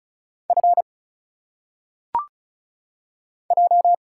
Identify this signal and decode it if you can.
Morse code, heard as a keyed tone, about 36 wpm, 700 Hz. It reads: F J
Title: Single Letters - Phonetic Pronunciation 35wpm